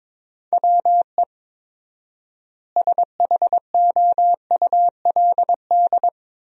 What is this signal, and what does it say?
Morse code, heard as a keyed tone, about 22 wpm, 700 Hz. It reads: WE SHOULD
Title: 4x Word-Spacing - Sentences from Top 300 Words - 22wpm